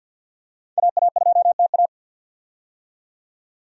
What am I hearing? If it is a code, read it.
Morse code, heard as a keyed tone, about 51 wpm, 700 Hz. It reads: AA2TA
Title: USA Call Signs - 5 Characters - 50wpm